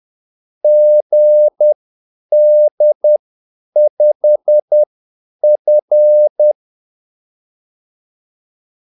Morse code, heard as a keyed tone, about 10 wpm, spaced as for 8 wpm, 600 Hz. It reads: GD5F
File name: Callsigns-10-8wpm